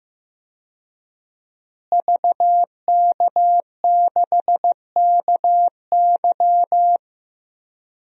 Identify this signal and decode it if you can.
Morse code, heard as a keyed tone, about 15 wpm, 700 Hz. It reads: VK6KY
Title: DXCC-100 Call Signs - 5 Characters - Repeated 2x - 15wpm